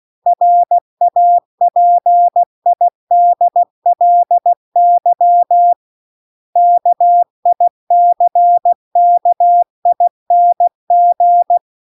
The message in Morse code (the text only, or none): RAPIDLY KICKING